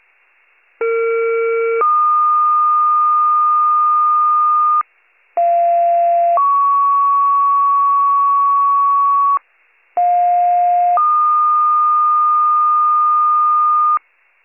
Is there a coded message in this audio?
Yes: two-tone page from 0.8 to 14.0 s, heard as a one-second tone, then a three-second tone, about 460 then 1220 Hz, about 690 then 1120 Hz, about 710 then 1220 Hz.